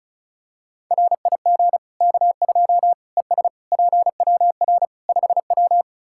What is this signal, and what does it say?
Morse code, heard as a keyed tone, about 35 wpm, 700 Hz. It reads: RIG K2 ES PWR 5W
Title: Practice QSO - Exchange 2 - Info - 35wpm